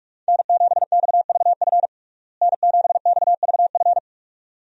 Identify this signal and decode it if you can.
Morse code, heard as a keyed tone, about 46 wpm, 700 Hz. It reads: N7XVF N7XVF
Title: USA Call Signs - 5 Characters - Repeated 2x - 45wpm